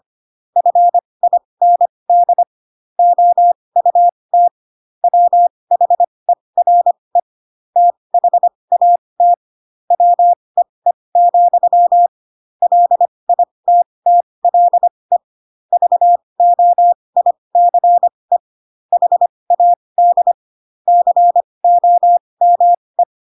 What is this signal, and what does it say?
Morse code, heard as a keyed tone, about 25 wpm, spaced as for 20 wpm, 700 Hz. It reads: FIND OUT WHERE THAT WEE, LITTLE VOICE HAD COME
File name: Pinocchio / Pinocchio ch1 15 25 ik1hge